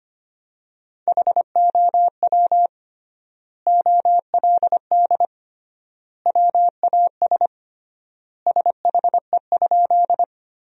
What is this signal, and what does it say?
Morse code, heard as a keyed tone, about 25 wpm, 700 Hz. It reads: HOW OLD WAS SHE?
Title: 3x Word-Spacing - Sentences from Top 200 Words - 25wpm